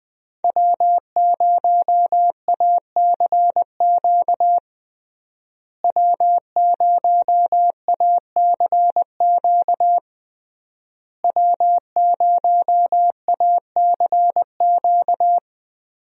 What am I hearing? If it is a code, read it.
Morse code, heard as a keyed tone, about 21 wpm, 700 Hz. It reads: W0ACQ W0ACQ W0ACQ
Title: USA Call Signs - 5 Characters - Repeated 3x - 20wpm